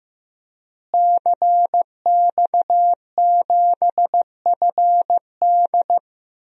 Morse code, heard as a keyed tone, about 15 wpm, 700 Hz. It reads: CX7FD
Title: DXCC-100 Call Signs - 5 Characters - Repeated 2x - 15wpm